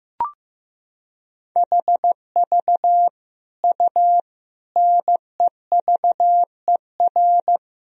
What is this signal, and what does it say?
Morse code, heard as a keyed tone, about 15 wpm, 700 Hz. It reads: HV U NEVER
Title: Sentences from Top 300 Words and Abbreviations 15wpm